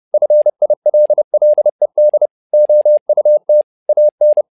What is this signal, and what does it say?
Morse code, heard as a keyed tone, about 30 wpm, 600 Hz. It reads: FILLED OUT AN
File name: learning center / tower trouble / tower trouble 30wpm